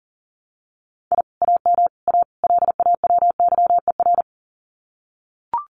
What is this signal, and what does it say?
Morse code, heard as a keyed tone, about 42 wpm, 700 Hz. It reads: I AM A LAWYER
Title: Common QSO Phrases Encore 40wpm